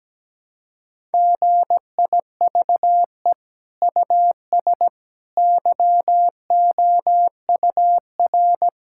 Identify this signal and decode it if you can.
Morse code, heard as a keyed tone, about 17 wpm, 700 Hz. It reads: GIVE US YOUR
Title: Sentences from Top 1000 Words - Review Part 5 - 17wpm